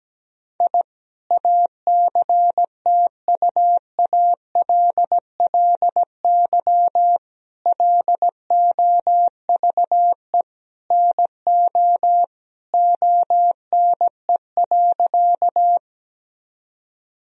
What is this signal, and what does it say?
Morse code, heard as a keyed tone, about 17 wpm, 700 Hz. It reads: I ACTUALLY LOVE NO ONE.
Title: Sentences from Top 1000 Words - Review Part 4 - 17wpm